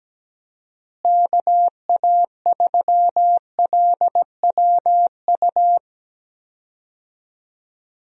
Morse code, heard as a keyed tone, about 17 wpm, 700 Hz. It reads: KA3LWU
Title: USA Call Signs - 6 Characters - Repeated 2x - 17wpm